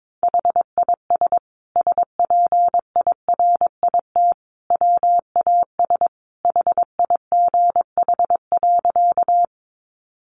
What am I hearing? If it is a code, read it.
Morse code, heard as a keyed tone, about 22 wpm, 700 Hz. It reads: HIS SPIRIT WAS HIGH.
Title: Sentences from Top 1400 Words 22wpm